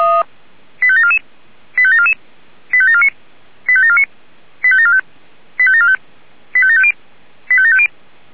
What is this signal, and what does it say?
Five-tone selective call (ZVEI), heard as quick runs of five tones, tones of about 70 ms. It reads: 85744 85744 85749 85749 85745 85745 85750 85750